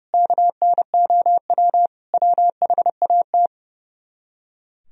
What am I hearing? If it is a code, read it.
Morse code, heard as a keyed tone, about 37 wpm, 700 Hz. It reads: KNOW WHAT
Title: Speed-Racing - Sets of 4 Words 30wpm